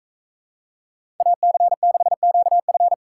Morse code, heard as a keyed tone, about 44 wpm, 700 Hz. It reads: AC6QF